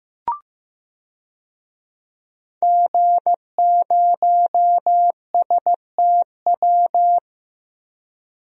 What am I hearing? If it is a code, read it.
Morse code, heard as a keyed tone, about 15 wpm, 700 Hz. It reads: G0STW
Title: DXCC-100 Call Signs - 5 Characters - Repeated 3x - 15wpm